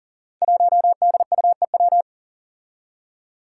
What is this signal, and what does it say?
Morse code, heard as a keyed tone, about 38 wpm, 700 Hz. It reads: 1DUEW
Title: Five Challenging Letter-Number Combinations 40wpm